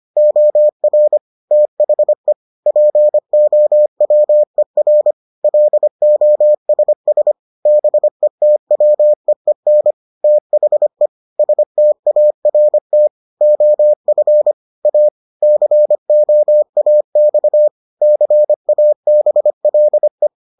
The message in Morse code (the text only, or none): OR THE POWER LOSS BETWEEN THE START OF A COAX CABLE